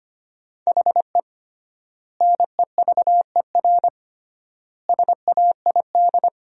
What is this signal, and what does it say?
Morse code, heard as a keyed tone, about 25 wpm, 700 Hz. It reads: HE NEVER SAID
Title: 3x Word-Spacing - Sentences from Top 500 Words - 25wpm